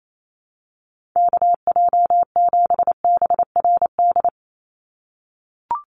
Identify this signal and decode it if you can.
Morse code, heard as a keyed tone, about 28 wpm, 700 Hz. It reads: KJ7BRD
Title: USA Call Signs 28wpm